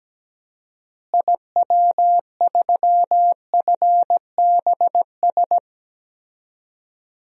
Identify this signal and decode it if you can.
Morse code, heard as a keyed tone, about 17 wpm, 700 Hz. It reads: IW3FBS
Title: DXCC-100 Call Signs - 6 Characters - Repeated 2x - 17wpm